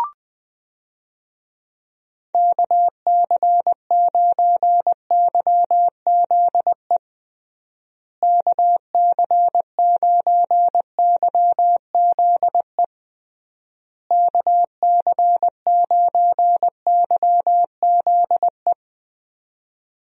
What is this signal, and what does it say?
Morse code, heard as a keyed tone, about 20 wpm, 700 Hz. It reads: KC9YZE KC9YZE KC9YZE
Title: USA Call Signs - 6 Characters - Repeated 3x - 20wpm